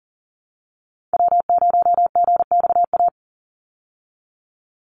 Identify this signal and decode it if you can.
Morse code, heard as a keyed tone, about 40 wpm, 700 Hz. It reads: W0GXA
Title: Call Signs Repeated 2x - 40wpm